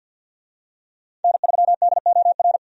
Morse code, heard as a keyed tone, about 48 wpm, 700 Hz. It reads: N3DOR
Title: USA Call Signs - 5 Characters - 50wpm